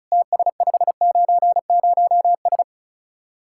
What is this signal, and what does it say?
Morse code, heard as a keyed tone, about 44 wpm, 700 Hz. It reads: TS590S
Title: Speed-Racing Common QSO Phrases Encore - 35wpm